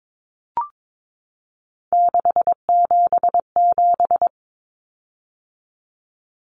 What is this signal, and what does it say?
Morse code, heard as a keyed tone, about 22 wpm, 700 Hz. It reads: 677
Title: Common Cut Numbers - 3-Digit 22wpm